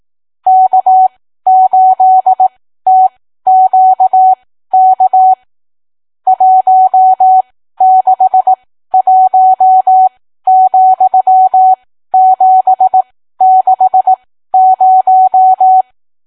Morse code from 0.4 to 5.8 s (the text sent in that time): K8TQK